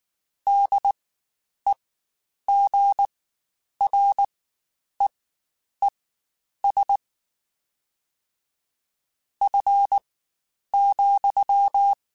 Morse code, heard as a keyed tone, about 19 wpm, spaced as for 9 wpm, 800 Hz. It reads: DEGREES F,
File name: Spaced Characters / Spaced Characters-mp3s / Weather 4 24 sp @18